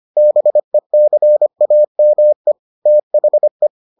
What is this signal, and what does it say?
Morse code, heard as a keyed tone, about 25 wpm, 600 Hz. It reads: BECAME THE